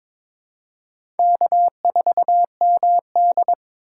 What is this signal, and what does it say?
Morse code, heard as a keyed tone, about 22 wpm, 700 Hz. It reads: K4MD
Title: USA Call Signs - 4x Character Spacing - 22wpm